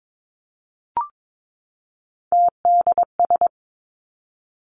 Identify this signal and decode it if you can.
Morse code, heard as a keyed tone, about 22 wpm, 700 Hz. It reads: TDS
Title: Three Challenging Letter-Number Combinations 22wpm